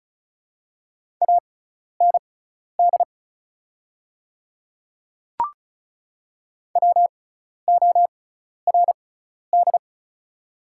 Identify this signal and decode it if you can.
Morse code, heard as a keyed tone, about 36 wpm, spaced as for 12 wpm, 700 Hz. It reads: AND WORD
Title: 6x Character Spacing - Top 100 Words - 35wpm